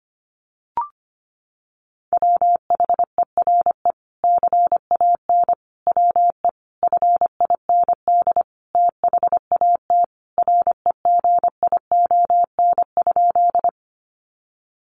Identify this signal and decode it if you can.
Morse code, heard as a keyed tone, about 25 wpm, 700 Hz. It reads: WHERE CAN WE FIND THAT REGION?
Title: Sentences from Top 1500 Words 25wpm